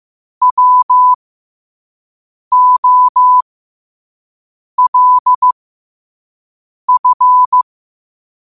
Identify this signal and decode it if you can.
Morse code, heard as a keyed tone, about 15 wpm, spaced as for 5 wpm, 1000 Hz. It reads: WOLF
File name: qotdfiles / 5wpm